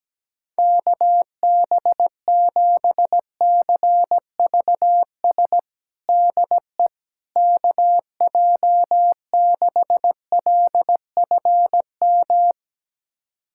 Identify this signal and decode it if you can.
Morse code, heard as a keyed tone, about 17 wpm, 700 Hz. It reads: KB7CVS DE KJ6LFM
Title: USA Callsign DE Callsign - 6 Characters - 17wpm